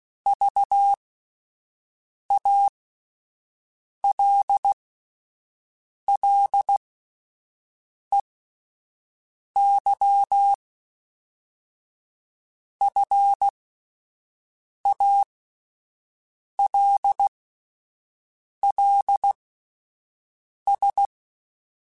Morse code, heard as a keyed tone, about 16 wpm, spaced as for 5 wpm, 800 Hz. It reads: VALLEY FALLS